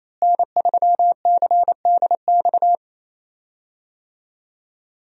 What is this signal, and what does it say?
Morse code, heard as a keyed tone, about 28 wpm, 700 Hz. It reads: N3CDX
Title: USA Call Signs - 5 Characters - Repeated 2x - 28wpm